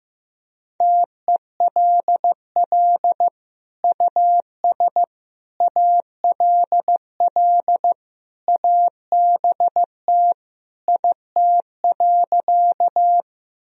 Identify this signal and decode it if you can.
Morse code, heard as a keyed tone, about 15 wpm, 700 Hz. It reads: TELL US ALL ABT IT.
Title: Sentences from Top 200 Words and Abbreviations 15wpm